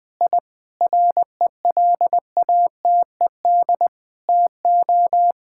Morse code, heard as a keyed tone, about 21 wpm, 700 Hz. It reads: I RELATED TO
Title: Sentences from Top 1000 Words - Review Part 5 - 20wpm